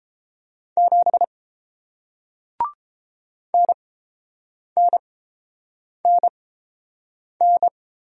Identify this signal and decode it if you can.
Morse code, heard as a keyed tone, about 30 wpm, spaced as for 7 wpm, 700 Hz. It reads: 7 NNNN